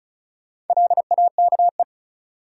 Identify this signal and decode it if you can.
Morse code, heard as a keyed tone, about 36 wpm, 700 Hz. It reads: LAKE
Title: Sentences from Top 1000 Words - Review Part 3 - New Word Drill - 35wpm